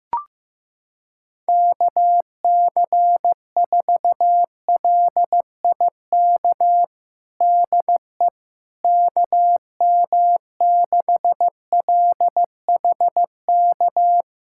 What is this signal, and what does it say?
Morse code, heard as a keyed tone, about 15 wpm, 700 Hz. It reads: KC4LIK DE KM6LHK
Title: USA Callsign DE Callsign - 6 Characters - 15wpm